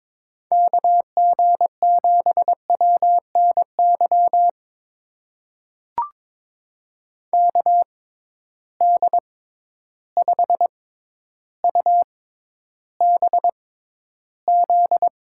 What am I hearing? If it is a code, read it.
Morse code, heard as a keyed tone, about 22 wpm, spaced as for 11 wpm, 700 Hz. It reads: KG7WNY KD5UBZ
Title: USA Call Signs - 6x Character Spacing - 22wpm